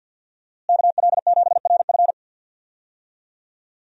Morse code, heard as a keyed tone, about 45 wpm, 700 Hz. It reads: KL7RF
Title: DXCC-100 Call Signs - 5 Characters - Repeated 2x - 50wpm